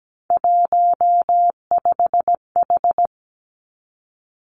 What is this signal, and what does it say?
Morse code, heard as a keyed tone, about 17 wpm, 700 Hz. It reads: 15H
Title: Three Challenging Letter-Number Combinations 17wpm